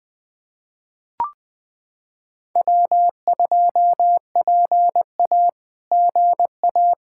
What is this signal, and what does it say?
Morse code, heard as a keyed tone, about 20 wpm, 700 Hz. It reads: W2PA GA
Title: SOTA QSO Practice 20wpm